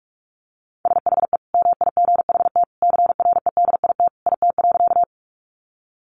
Morse code, heard as a keyed tone, about 44 wpm, 700 Hz. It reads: SHE MIGHT CREDIT IT.
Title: Sentences from Top 1000 Words 45wpm